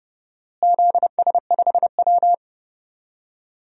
Morse code, heard as a keyed tone, about 30 wpm, 700 Hz. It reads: ZS5W
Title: DXCC-100 Call Signs - 4 Characters - Repeated 2x - 30wpm